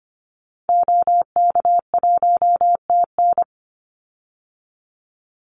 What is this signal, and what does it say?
Morse code, heard as a keyed tone, about 25 wpm, 700 Hz. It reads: OK1TN
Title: Call Signs Repeated 3x - 25wpm